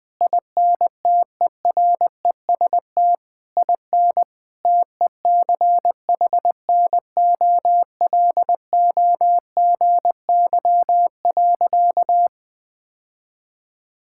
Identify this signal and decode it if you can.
Morse code, heard as a keyed tone, about 21 wpm, 700 Hz. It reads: INTEREST IN TECHNOLOGY.
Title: Sentences from Top 1000 Words - Review Part 5 - 20wpm